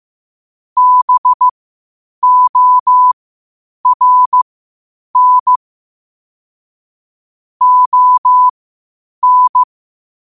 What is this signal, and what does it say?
Morse code, heard as a keyed tone, about 15 wpm, spaced as for 8 wpm, 1000 Hz. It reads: BORN ON
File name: 8wpm